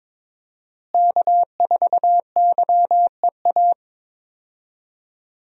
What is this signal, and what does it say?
Morse code, heard as a keyed tone, about 22 wpm, 700 Hz. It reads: K4YEA
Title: USA Call Signs - 5 Characters - Repeated 3x - 22wpm